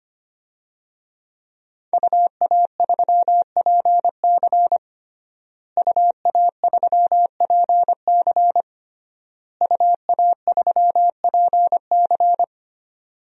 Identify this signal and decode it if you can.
Morse code, heard as a keyed tone, about 25 wpm, 700 Hz. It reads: UA3PC UA3PC UA3PC
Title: DXCC-100 Call Signs - 5 Characters - Repeated 3x - 25wpm